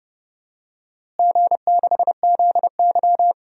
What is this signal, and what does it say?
Morse code, heard as a keyed tone, about 30 wpm, 700 Hz. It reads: G6ZY
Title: DXCC-100 Call Signs - 4 Characters - Repeated 2x - 30wpm